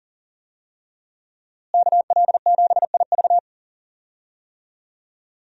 Morse code, heard as a keyed tone, about 40 wpm, 700 Hz. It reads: KL7IV
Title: USA Call Signs - 5 Characters - Repeated 3x - 40wpm